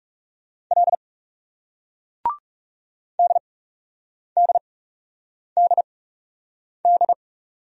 Morse code, heard as a keyed tone, about 39 wpm, spaced as for 8 wpm, 700 Hz. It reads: L DDDD